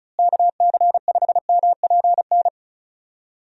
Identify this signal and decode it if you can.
Morse code, heard as a keyed tone, about 36 wpm, 700 Hz. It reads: KC5MPN
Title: USA Call Signs - 6 Characters - Repeated 2x - 35wpm